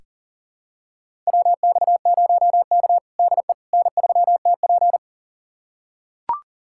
Morse code, heard as a keyed tone, about 40 wpm, 700 Hz. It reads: WX0K DE N3TP
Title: USA Callsign DE Callsign - 4 Characters - 40wpm